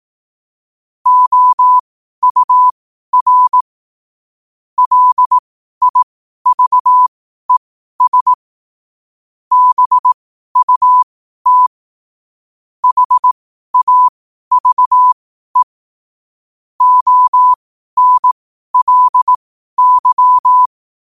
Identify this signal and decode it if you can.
Morse code, heard as a keyed tone, about 18 wpm, spaced as for 12 wpm, 1000 Hz. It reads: OUR LIVES BUT HAVE ONLY
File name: qotdfiles / 12wpm